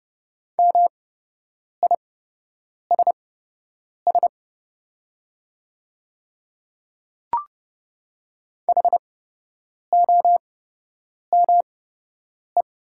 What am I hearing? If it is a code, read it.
Morse code, heard as a keyed tone, about 31 wpm, spaced as for 8 wpm, 690 Hz. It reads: MISS HOME
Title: 8x Character Spacing - Top 500 Words - 30wpm